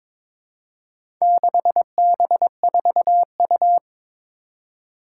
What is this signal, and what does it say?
Morse code, heard as a keyed tone, about 21 wpm, 700 Hz. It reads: 6B4U
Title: Four Challenging Letter-Number Combinations 22wpm